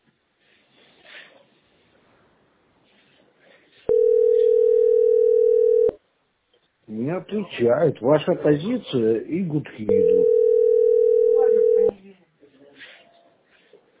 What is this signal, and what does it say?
Ringback tone (North America), 440 + 480 Hz, with speech.
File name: 3fbcf1ec-a99d-49b7-8a79